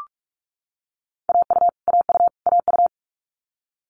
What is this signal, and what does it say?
Morse code, heard as a keyed tone, about 46 wpm, 700 Hz. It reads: AU AU AU
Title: Multi-letter Phonemes Repeated 3x 45wpm